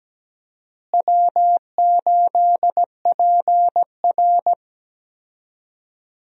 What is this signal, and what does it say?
Morse code, heard as a keyed tone, about 17 wpm, 700 Hz. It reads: W8PR
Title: USA Call Signs - 4 Characters - Repeated 2x - 17wpm